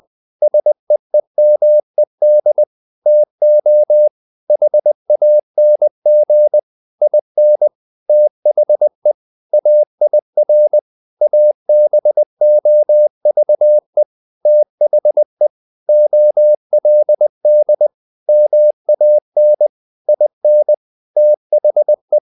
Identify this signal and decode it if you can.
Morse code, heard as a keyed tone, about 20 wpm, 600 Hz. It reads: SEEMED TO HANG IN THE AIR ABOVE THE OLD MAN IN THE
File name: the old man and the sea part 2 20wpm